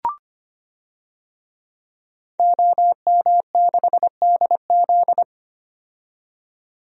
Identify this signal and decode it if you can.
Morse code, heard as a keyed tone, about 25 wpm, 700 Hz. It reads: OM6DZ
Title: DXCC-100 Call Signs - 5 Characters - 25wpm